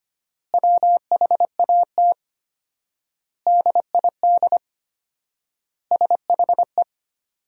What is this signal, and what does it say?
Morse code, heard as a keyed tone, about 25 wpm, 700 Hz. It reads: WHAT DID SHE
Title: 4x Word-Spacing - Sentences from Top 500 Words - 25wpm